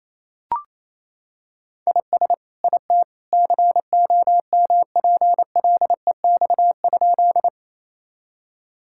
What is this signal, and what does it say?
Morse code, heard as a keyed tone, about 28 wpm, 700 Hz. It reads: IS IT COMPLEX?
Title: Sentences from Top 1000 Words - Review Part 6 - 28wpm